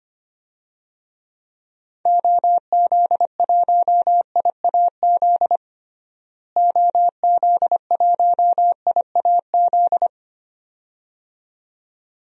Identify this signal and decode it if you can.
Morse code, heard as a keyed tone, about 25 wpm, 700 Hz. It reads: OZ1IAZ OZ1IAZ
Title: DXCC-100 Call Signs - 6 Characters - Repeated 2x - 25wpm